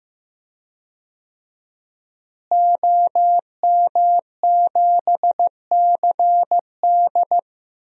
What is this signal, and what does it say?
Morse code, heard as a keyed tone, about 15 wpm, 700 Hz. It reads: OM7CD
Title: DXCC-100 Call Signs - 5 Characters - Repeated 3x - 15wpm